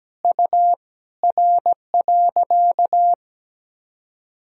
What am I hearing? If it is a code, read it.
Morse code, heard as a keyed tone, about 17 wpm, 700 Hz. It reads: U R.
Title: Sentences from Top 500 Words and Abbreviations 17wpm